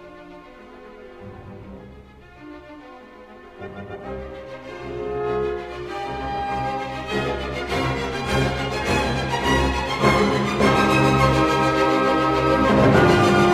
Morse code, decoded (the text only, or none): none